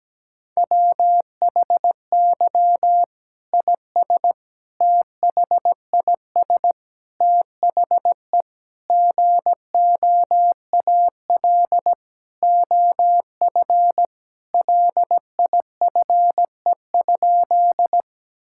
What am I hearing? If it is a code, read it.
Morse code, heard as a keyed tone, about 17 wpm, 700 Hz. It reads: WHY IS THIS THE GOAL OF LIFE?
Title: Sentences from Top 1000 Words - Review Part 5 - 17wpm